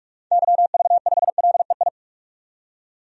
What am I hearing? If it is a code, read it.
Morse code, heard as a keyed tone, about 46 wpm, 700 Hz. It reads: YV5LEI
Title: DXCC-100 Call Signs - 6 Characters - 45wpm